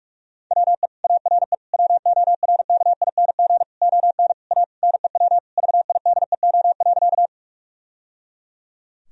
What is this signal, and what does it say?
Morse code, heard as a keyed tone, about 41 wpm, 700 Hz. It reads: WE ARE WORKING ON A NEW VIDEO.
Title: Sentences from Top 1000 Words - Review Part 4 - 45wpm